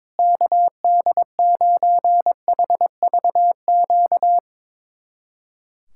Morse code, heard as a keyed tone, about 22 wpm, 700 Hz. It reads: KD9HVQ
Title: USA Callsign DE Callsign - 6 Characters - 22wpm